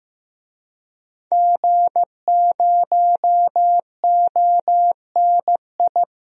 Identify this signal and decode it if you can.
Morse code, heard as a keyed tone, about 15 wpm, 700 Hz. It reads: G0ONI
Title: DXCC-100 Call Signs - 5 Characters - 15wpm